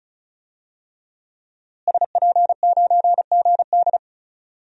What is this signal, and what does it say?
Morse code, heard as a keyed tone, about 34 wpm, 700 Hz. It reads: SP9GD
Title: DXCC-100 Call Signs - 5 Characters - Repeated 3x - 35wpm